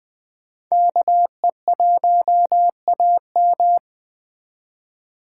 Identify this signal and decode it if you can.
Morse code, heard as a keyed tone, about 20 wpm, 700 Hz. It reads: KE1AM